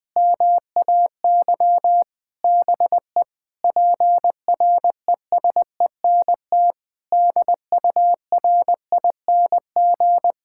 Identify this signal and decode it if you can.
Morse code, heard as a keyed tone, about 20 wpm, 700 Hz. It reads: MAY BE PRESENT DURING
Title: Sentences from Top 1600 Words 20wpm